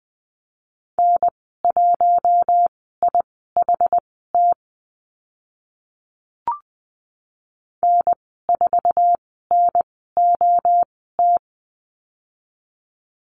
Morse code, heard as a keyed tone, about 20 wpm, spaced as for 14 wpm, 700 Hz. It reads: N1IHT N4NOT
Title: 062 Lesson 11g - Letter h - Call Signs 20wpm